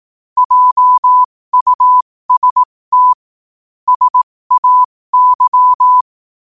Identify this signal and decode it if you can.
Morse code, heard as a keyed tone, about 18 wpm, spaced as for 15 wpm, 1000 Hz. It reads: JUST SAY